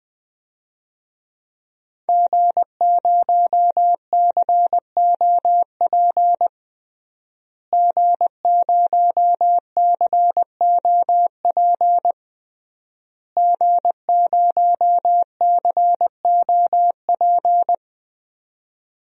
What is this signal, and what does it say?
Morse code, heard as a keyed tone, about 20 wpm, 700 Hz. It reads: G0COP G0COP G0COP
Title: DXCC-100 Call Signs - 5 Characters - Repeated 3x - 20wpm